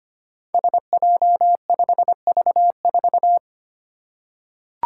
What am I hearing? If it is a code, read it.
Morse code, heard as a keyed tone, about 25 wpm, 700 Hz. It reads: SJ5V4
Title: Five Challenging Letter-Number Combinations 25wpm